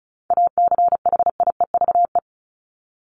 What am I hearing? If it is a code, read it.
Morse code, heard as a keyed tone, about 34 wpm, 700 Hz. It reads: ACHIEVE
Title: Top 751-1000 Words 35wpm